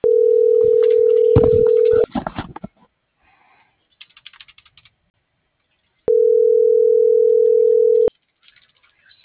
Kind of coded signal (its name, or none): telephone call progress tone